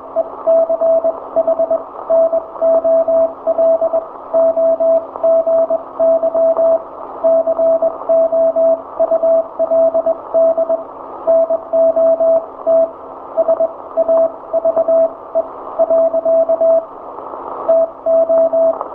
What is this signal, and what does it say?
Morse code, heard as a keyed tone, about 22 wpm, spaced as for 18 wpm, 660 Hz, with radio noise. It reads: ECHNOLOGY COULD NOT SAVE . TO
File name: hamradio / last 500khz cw